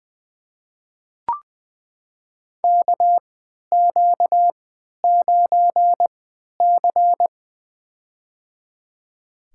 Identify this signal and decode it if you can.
Morse code, heard as a keyed tone, about 21 wpm, spaced as for 11 wpm, 700 Hz. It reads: KQ9C
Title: USA Call Signs - 4 Characters - 3x Character Spacing - 20wpm